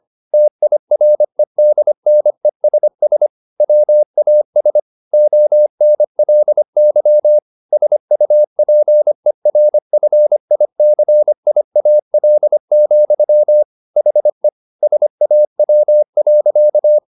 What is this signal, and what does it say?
Morse code, heard as a keyed tone, about 25 wpm, 600 Hz. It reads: TIREDNESS WAS ONLY SUPERFICIAL, HE SAW.